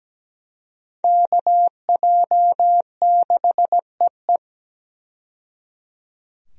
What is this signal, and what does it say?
Morse code, heard as a keyed tone, about 17 wpm, 700 Hz. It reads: KJ6EE
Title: USA Call Signs - 5 Characters - 17wpm